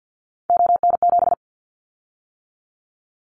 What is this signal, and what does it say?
Morse code, heard as a keyed tone, about 50 wpm, 700 Hz. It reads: ON7
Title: Callsign Prefixes for Top 100 DXCC - 50wpm